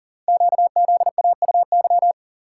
Morse code, heard as a keyed tone, about 40 wpm, 700 Hz. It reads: QZAUY
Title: Code Groups - 5 Letters - 40wpm